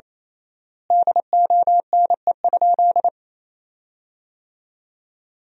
Morse code, heard as a keyed tone, about 28 wpm, 700 Hz. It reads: DONE?
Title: 3x Word-Spacing - Sentences from Top 500 Words - 28wpm